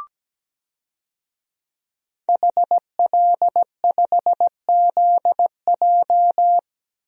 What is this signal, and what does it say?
Morse code, heard as a keyed tone, about 17 wpm, 700 Hz. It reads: HL5ZJ
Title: DXCC-100 Call Signs - 5 Characters - 17wpm